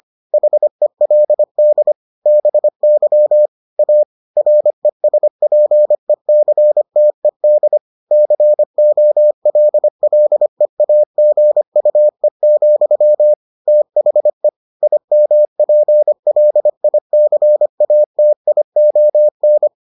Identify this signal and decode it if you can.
Morse code, heard as a keyed tone, about 25 wpm, 600 Hz. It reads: HELD BY A RESPECTED COLLEAGUE, THE IMPLICATION